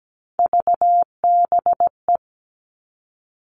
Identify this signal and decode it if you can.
Morse code, heard as a keyed tone, about 17 wpm, 700 Hz. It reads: VBE
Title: Three Challenging Letter-Number Combinations 17wpm